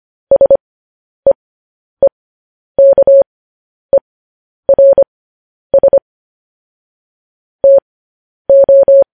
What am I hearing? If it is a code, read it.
Morse code, heard as a keyed tone, about 25 wpm, spaced as for 10 wpm, 550 Hz. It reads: SEEKERS TO
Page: News Headlines in Morse Code at 10 WPM